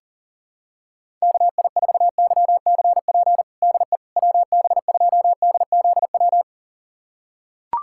Morse code, heard as a keyed tone, about 40 wpm, 700 Hz. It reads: KI4YCP DE WB2DZW
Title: USA Callsign DE Callsign - 6 Characters - 40wpm